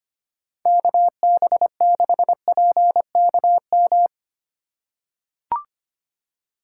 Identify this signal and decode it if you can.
Morse code, heard as a keyed tone, about 25 wpm, 700 Hz. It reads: KB6PKM
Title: USA Call Signs - 6 Characters - 3x Character Spacing - 25wpm